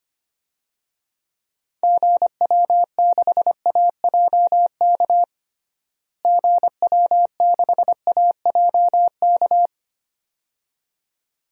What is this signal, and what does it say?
Morse code, heard as a keyed tone, about 25 wpm, 700 Hz. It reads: GW6AJK GW6AJK
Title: DXCC-100 Call Signs - 6 Characters - Repeated 2x - 25wpm